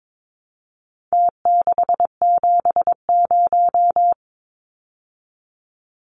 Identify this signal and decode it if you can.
Morse code, heard as a keyed tone, about 22 wpm, 700 Hz. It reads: T670
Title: Numbers - Contest Style - 4-Digit 22wpm